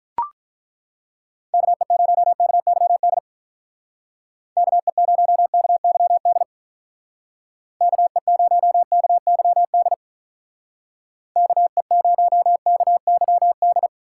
Morse code, heard as a keyed tone, about 42 wpm, 700 Hz. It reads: KE0KYD KE0KYD KE0KYD KE0KYD
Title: Speed Racing - USA 2x3 Call Signs 35wpm